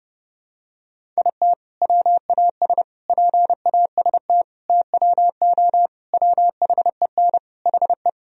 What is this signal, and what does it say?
Morse code, heard as a keyed tone, about 32 wpm, 700 Hz. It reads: IT WAS PAST TWO WHEN HE
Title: Sentences from Top 1000 Words - Review Part 3 - 30wpm